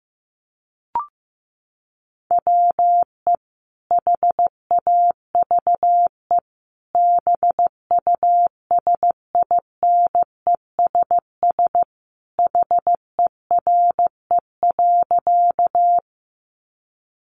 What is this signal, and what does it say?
Morse code, heard as a keyed tone, about 15 wpm, 700 Hz. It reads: WE HAVE BUSINESS HERE.
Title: Sentences from Top 600 Words 15wpm